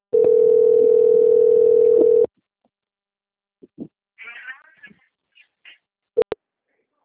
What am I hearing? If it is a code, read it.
Ringback tone (North America), 440 + 480 Hz.
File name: out- - - - - .19913